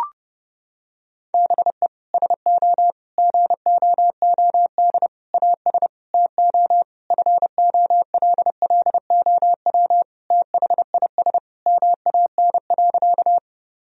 Morse code, heard as a keyed tone, about 32 wpm, 700 Hz. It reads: BE SO GOOD AS TO FOLLOW THIS MAN.
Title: Sentences from Top 1000 Words - Review Part 1 - 30wpm